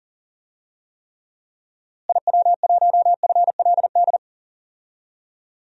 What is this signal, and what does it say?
Morse code, heard as a keyed tone, about 40 wpm, 700 Hz. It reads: IW1FLD
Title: DXCC-100 Call Signs - 6 Characters - Repeated 3x - 40wpm